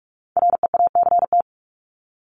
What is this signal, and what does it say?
Morse code, heard as a keyed tone, about 44 wpm, 700 Hz. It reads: REACT
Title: Top 1251-1500 Words 45wpm